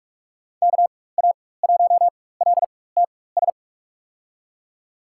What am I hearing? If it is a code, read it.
Morse code, heard as a keyed tone, about 47 wpm, spaced as for 21 wpm, 700 Hz. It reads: KA1LTS
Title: USA Call Signs - 6 Characters - 4x Character Spacing - 45wpm